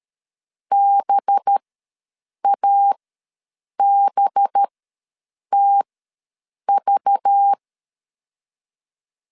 Morse code, heard as a keyed tone, about 13 wpm, spaced as for 7 wpm, 800 Hz. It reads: BABTV
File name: LEZIONI / 5 LBVC / 5 LBV ws 12wpm 3015